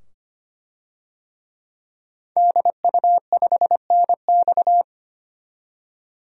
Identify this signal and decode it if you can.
Morse code, heard as a keyed tone, about 25 wpm, 700 Hz. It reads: DU5NX
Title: DXCC-100 Call Signs - 5 Characters - 25wpm